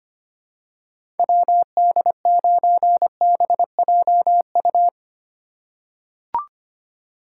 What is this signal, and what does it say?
Morse code, heard as a keyed tone, about 25 wpm, 700 Hz. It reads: WD9BJU
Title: USA Call Signs - 6 Characters - 4x Character Spacing - 25wpm